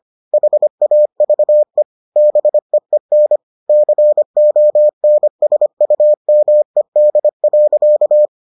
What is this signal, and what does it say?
Morse code, heard as a keyed tone, about 25 wpm, 600 Hz. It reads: HAVE BEEN CONSUMED.